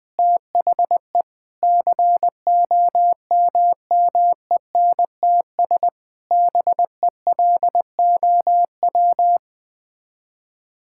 Morse code, heard as a keyed tone, about 20 wpm, 700 Hz. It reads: THE COMMENTS BELOW
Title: Sets of 9 Words - 20wpm